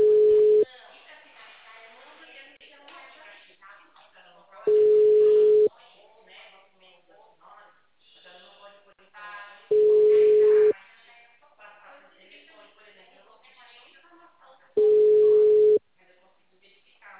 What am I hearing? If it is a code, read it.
Ringback tone (Europe, 425 Hz), with speech.